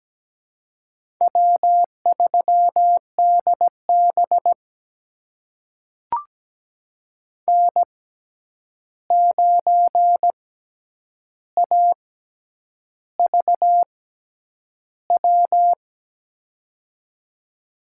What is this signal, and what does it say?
Morse code, heard as a keyed tone, about 17 wpm, spaced as for 8 wpm, 700 Hz. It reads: W3DB N9AVW
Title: USA Call Signs - 6x Character Spacing - 17wpm